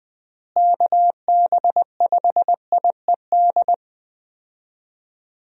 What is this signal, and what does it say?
Morse code, heard as a keyed tone, about 20 wpm, 700 Hz. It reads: KB5IED
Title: USA Call Signs - 6 Characters - Repeated 2x - 20wpm